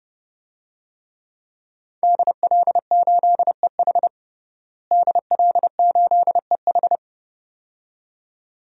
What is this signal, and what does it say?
Morse code, heard as a keyed tone, about 30 wpm, 700 Hz. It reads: DL8EH DL8EH
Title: DXCC-100 Call Signs - 5 Characters - Repeated 2x - 30wpm